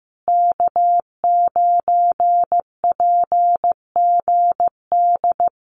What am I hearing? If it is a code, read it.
Morse code, heard as a keyed tone, about 15 wpm, 700 Hz. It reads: K9PGD
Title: Common QSO Phrases Encore 15wpm